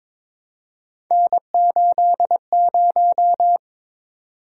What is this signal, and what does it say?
Morse code, heard as a keyed tone, about 22 wpm, 700 Hz. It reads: N80
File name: practice / 22wpm / Three Letter-Number Combinations - 22wpm